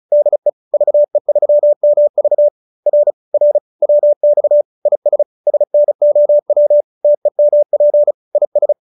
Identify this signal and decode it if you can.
Morse code, heard as a keyed tone, about 34 wpm, 590 Hz. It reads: DE VE3MV R R WX IS SNOW TEMP IS